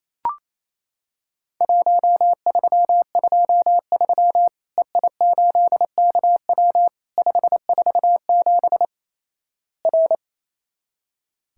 Morse code, heard as a keyed tone, about 28 wpm, 700 Hz. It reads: 1323 EI8KW 547 R
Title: Worked All Europe DX Contest - QTC Exchange - 28wpm